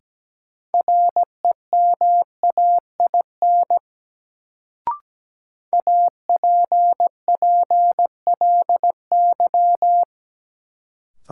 Morse code, heard as a keyed tone, about 17 wpm, 700 Hz. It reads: REMAIN APPLY
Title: Sentences from Top 1200 Words - New Word Drill - 17wpm